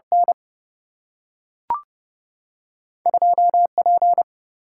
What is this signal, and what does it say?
Morse code, heard as a keyed tone, about 29 wpm, 700 Hz. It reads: N 2 P
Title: Single-Letter Single-Number 30wpm